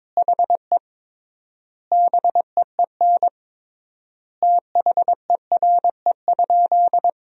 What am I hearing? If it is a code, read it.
Morse code, heard as a keyed tone, about 22 wpm, 700 Hz. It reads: HE BEEN THERE?
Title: 3x Word-Spacing - Sentences from Top 100 Words - 22wpm